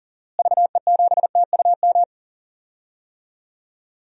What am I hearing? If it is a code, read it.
Morse code, heard as a keyed tone, about 40 wpm, 700 Hz. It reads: VE7TUM
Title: DXCC-100 Call Signs - 6 Characters - Repeated 2x - 40wpm